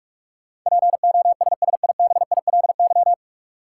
Morse code, heard as a keyed tone, about 46 wpm, 700 Hz. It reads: POSSIBILY
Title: Sentences from Top 1300 Words - New Word Drill Alt - 45wpm